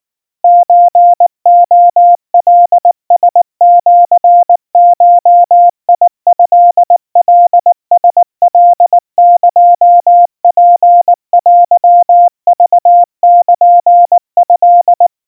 Morse code (Japanese Wabun code), heard as a keyed tone, about 19 wpm, 700 Hz. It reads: ソレカラシゴトカラカエツテクルト
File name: kawa / Morse sample / kamyu 70 cpm 700 Hz Crispy